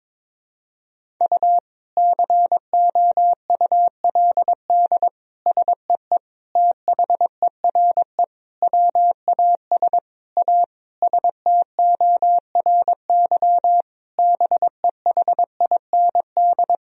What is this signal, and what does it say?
Morse code, heard as a keyed tone, about 22 wpm, 700 Hz. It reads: U COULD SEE THERE WAS A STORY BEHIND
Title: Sentences from Top 500 Words and Abbreviations 22wpm